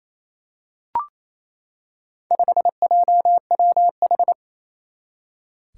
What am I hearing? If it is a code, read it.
Morse code, heard as a keyed tone, about 28 wpm, 700 Hz. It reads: 5JWH
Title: Four Challenging Letter-Number Combinations 28wpm